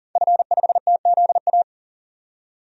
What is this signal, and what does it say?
Morse code, heard as a keyed tone, about 40 wpm, 700 Hz. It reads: F5TZA
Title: DXCC-100 Call Signs - 5 Characters - 40wpm